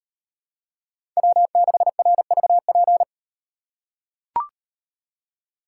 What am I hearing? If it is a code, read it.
Morse code, heard as a keyed tone, about 35 wpm, 700 Hz. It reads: W6RVP